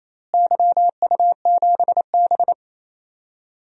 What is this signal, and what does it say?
Morse code, heard as a keyed tone, about 28 wpm, 700 Hz. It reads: YU7B
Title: DXCC-100 Call Signs - 4 Characters - Repeated 3x - 28wpm